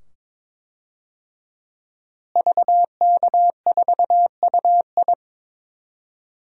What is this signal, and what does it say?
Morse code, heard as a keyed tone, about 22 wpm, 700 Hz. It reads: VK4UI
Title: DXCC-100 Call Signs - 5 Characters - Repeated 2x - 22wpm